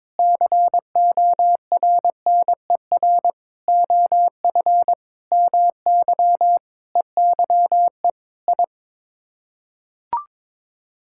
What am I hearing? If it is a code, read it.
Morse code, heard as a keyed tone, about 22 wpm, 700 Hz. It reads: CORNER OF MY EYE I
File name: practice / 22wpm / Sets of 8 Words - 22wpm